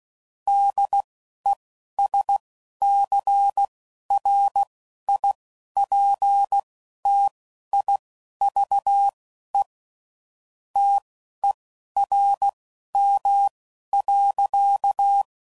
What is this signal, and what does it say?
Morse code, heard as a keyed tone, about 16 wpm, spaced as for 11 wpm, 800 Hz. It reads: DESCRIPTIVE TERM.
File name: CW Code Groups @10WPM / Technical Subjects / Reactance